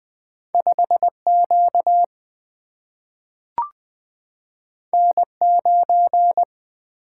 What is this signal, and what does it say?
Morse code, heard as a keyed tone, about 20 wpm, 700 Hz. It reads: 5Q N9
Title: Single-Letter Single-Number 20wpm